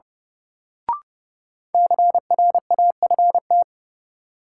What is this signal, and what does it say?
Morse code, heard as a keyed tone, about 30 wpm, 700 Hz. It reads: CRAFT
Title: Sentences from Top 1000 Words - Review Part 6 - New Word Drill - 30wpm